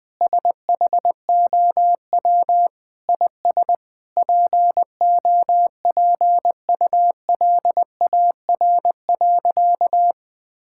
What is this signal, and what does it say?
Morse code, heard as a keyed tone, about 20 wpm, 700 Hz. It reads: SHOW IS POPULAR.
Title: Sentences with 6-Word Phrases - 20wpm